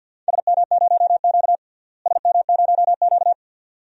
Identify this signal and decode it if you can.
Morse code, heard as a keyed tone, about 51 wpm, 700 Hz. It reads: SM0Q SM0Q
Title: DXCC-100 Call Signs - 4 Characters - Repeated 2x - 50wpm